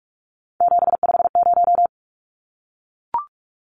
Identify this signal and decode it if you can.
Morse code, heard as a keyed tone, about 44 wpm, 700 Hz. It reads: 750
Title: Numbers - 3-Digit 45wpm